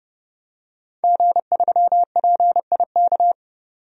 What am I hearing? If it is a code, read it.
Morse code, heard as a keyed tone, about 30 wpm, 700 Hz. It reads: G3PIK
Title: DXCC-100 Call Signs - 5 Characters - 30wpm